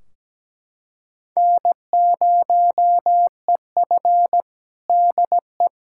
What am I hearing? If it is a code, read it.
Morse code, heard as a keyed tone, about 17 wpm, 700 Hz. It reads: N0EF DE
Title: Practice QSO - Exchange 4 - Ending - 17wpm